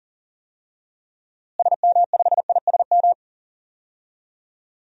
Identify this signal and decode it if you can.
Morse code, heard as a keyed tone, about 40 wpm, 700 Hz. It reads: SM5ISM
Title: DXCC-100 Call Signs - 6 Characters - Repeated 3x - 40wpm